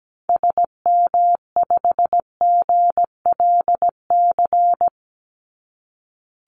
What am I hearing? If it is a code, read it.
Morse code, heard as a keyed tone, about 17 wpm, 700 Hz. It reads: SM5GLC
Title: Call Signs - 17wpm